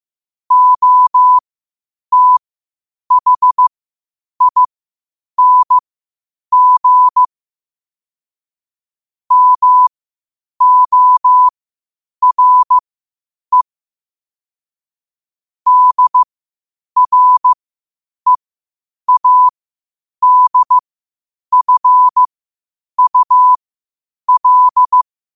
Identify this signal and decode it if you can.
Morse code, heard as a keyed tone, about 15 wpm, spaced as for 8 wpm, 1000 Hz. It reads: OTHING MORE DREADFUL
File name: qotdfiles / 8wpm